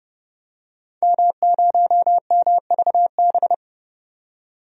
Morse code, heard as a keyed tone, about 30 wpm, 700 Hz. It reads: M0MVB
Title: DXCC-100 Call Signs - 5 Characters - 30wpm